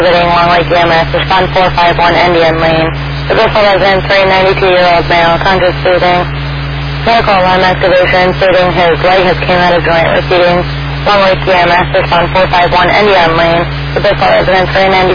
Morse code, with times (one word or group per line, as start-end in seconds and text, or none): none